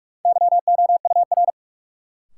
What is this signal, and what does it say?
Morse code, heard as a keyed tone, about 44 wpm, 700 Hz. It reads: YOUR
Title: Sentences from Top 100 Words - New Word Drill - 45wpm